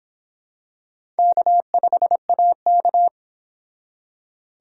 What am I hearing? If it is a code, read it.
Morse code, heard as a keyed tone, about 25 wpm, 700 Hz. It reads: K5AK